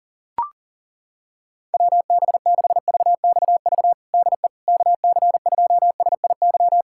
Morse code, heard as a keyed tone, about 40 wpm, 700 Hz. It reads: WB6VXV DE KC2SIY
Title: USA Callsign DE Callsign 40wpm